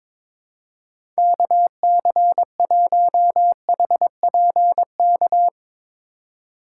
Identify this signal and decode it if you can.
Morse code, heard as a keyed tone, about 22 wpm, 700 Hz. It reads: KC1HPK
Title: USA Call Signs - 6 Characters - 3x Character Spacing - 22wpm